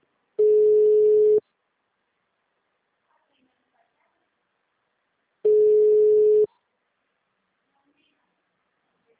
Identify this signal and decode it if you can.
Ringback tone (Europe, 425 Hz), with speech.